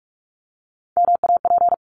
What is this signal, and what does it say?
Morse code, heard as a keyed tone, about 46 wpm, 700 Hz. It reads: MAP
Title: Top 500 Words 45wpm